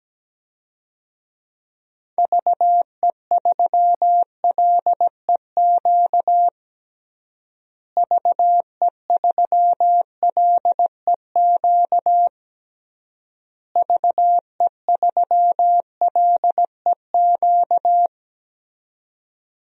Morse code, heard as a keyed tone, about 17 wpm, 700 Hz. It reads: VE3LEQ VE3LEQ VE3LEQ
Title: DXCC-100 Call Signs - 6 Characters - Repeated 3x - 17wpm